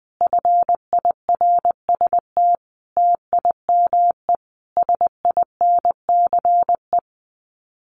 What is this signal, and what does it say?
Morse code, heard as a keyed tone, about 20 wpm, 700 Hz. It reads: FIRST TIME SINCE
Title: Sets of 5 Words 20wpm